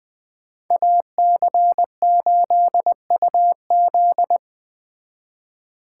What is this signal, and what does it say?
Morse code, heard as a keyed tone, about 20 wpm, 700 Hz. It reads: AC8UZ
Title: Speed Racing - USA 2x2 Call Signs 17wpm